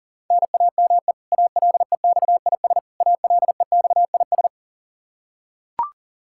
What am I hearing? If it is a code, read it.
Morse code, heard as a keyed tone, about 40 wpm, 700 Hz. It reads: NAME ALEXIS ALEXIS
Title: Practice QSO - Exchange 1 - Beginning - 40wpm